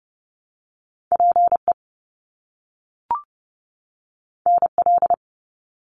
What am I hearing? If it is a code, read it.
Morse code, heard as a keyed tone, about 30 wpm, 700 Hz. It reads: PE NL
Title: Canadian Provinces and Territories 30wpm